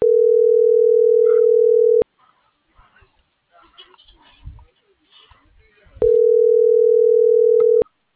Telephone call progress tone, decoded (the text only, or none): ringback tone